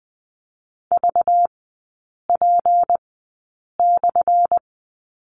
Morse code, heard as a keyed tone, about 20 wpm, spaced as for 8 wpm, 690 Hz. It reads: VP/
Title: 191 Lesson 31e - Letter v - Three Characters 20wpm